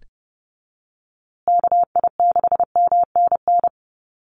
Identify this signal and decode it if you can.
Morse code, heard as a keyed tone, about 30 wpm, 700 Hz. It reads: KI6MNN
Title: USA Call Signs Repeated 2x 30wpm